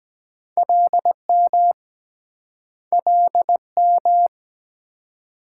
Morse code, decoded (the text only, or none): LM LM